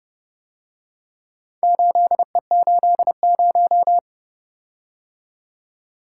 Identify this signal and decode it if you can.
Morse code, heard as a keyed tone, about 30 wpm, 700 Hz. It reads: 8E80